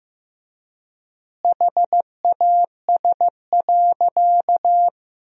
Morse code, heard as a keyed tone, about 16 wpm, 700 Hz. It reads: HAS.
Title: 3x Word-Spacing - Sentences from Top 500 Words - 15wpm